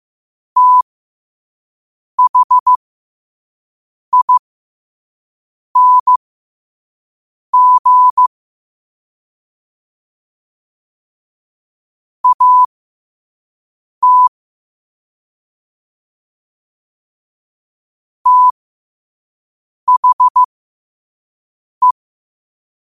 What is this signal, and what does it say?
Morse code, heard as a keyed tone, about 15 wpm, spaced as for 5 wpm, 1000 Hz. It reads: THING AT THE
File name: qotdfiles / 5wpm